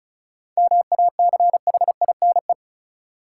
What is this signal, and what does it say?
Morse code, heard as a keyed tone, about 36 wpm, 700 Hz. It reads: MACHINE
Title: Sentences from Top 1000 Words - Review Part 2 - New Word Drill - 35wpm